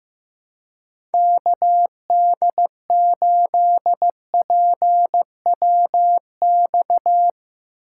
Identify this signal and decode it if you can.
Morse code, heard as a keyed tone, about 15 wpm, 700 Hz. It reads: KD8PWX
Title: USA Call Signs - 6 Characters - 2x Character Spacing - 15wpm